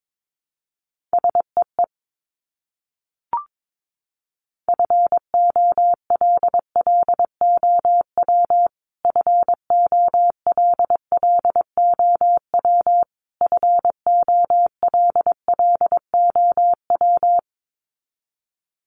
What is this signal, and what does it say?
Morse code, heard as a keyed tone, about 22 wpm, 700 Hz. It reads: SEE FOLLOW FOLLOW FOLLOW
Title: Top 500 Words Repeated 22wpm